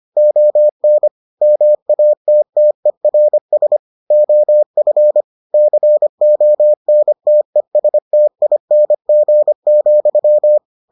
Morse code, heard as a keyed tone, about 25 wpm, 600 Hz. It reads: ON MATTERS OF CONTESTING,